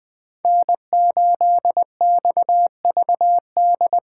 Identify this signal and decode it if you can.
Morse code, heard as a keyed tone, about 24 wpm, 700 Hz. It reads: N8XVD
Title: Speed Racing - USA 1x3 Call Signs 20wpm